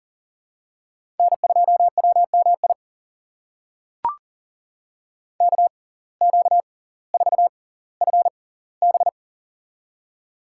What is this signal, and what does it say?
Morse code, heard as a keyed tone, about 37 wpm, spaced as for 20 wpm, 700 Hz. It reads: N2WMI KQ4FB
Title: USA Call Signs - 5 Characters - 6x Character Spacing - 40wpm